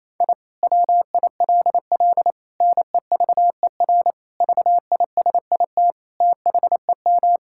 Morse code, heard as a keyed tone, about 28 wpm, 700 Hz. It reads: I WILL NEVER VISIT THEM
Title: Sentences from Top 1000 Words - Review Part 6 - 28wpm